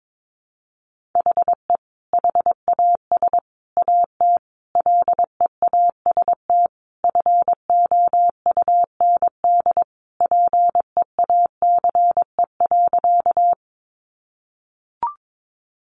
Morse code, heard as a keyed tone, about 22 wpm, 700 Hz. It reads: HE HAS AT LEAST FOUND PEACE.
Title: Sentences from Top 1500 Words 22wpm